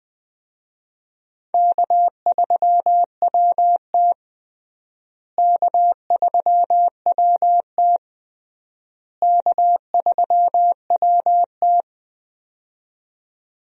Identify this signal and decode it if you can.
Morse code, heard as a keyed tone, about 20 wpm, 700 Hz. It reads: K3WT K3WT K3WT
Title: USA Call Signs - 4 Characters - Repeated 3x - 20wpm